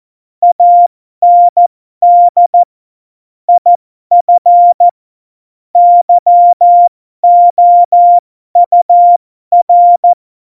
Morse code, heard as a keyed tone, about 14 wpm, spaced as for 12 wpm, 700 Hz. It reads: AND IF YOUR